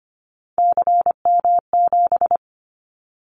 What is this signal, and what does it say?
Morse code, heard as a keyed tone, about 25 wpm, 700 Hz. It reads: CM7
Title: Callsign Prefixes for Top 100 DXCC - 25wpm